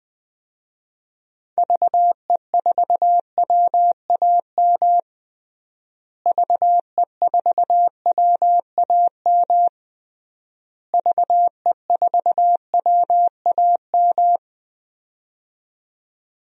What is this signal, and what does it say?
Morse code, heard as a keyed tone, about 21 wpm, 700 Hz. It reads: VE4WAM VE4WAM VE4WAM
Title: DXCC-100 Call Signs - 6 Characters - Repeated 3x - 20wpm